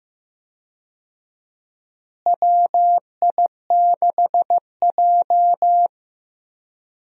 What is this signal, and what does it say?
Morse code, heard as a keyed tone, about 15 wpm, 700 Hz. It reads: WI6J
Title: USA Call Signs - 4 Characters - Repeated 2x - 15wpm